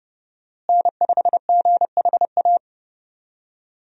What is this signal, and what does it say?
Morse code, heard as a keyed tone, about 30 wpm, 700 Hz. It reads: N5GHA
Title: USA Call Signs - 5 Characters - Repeated 2x - 30wpm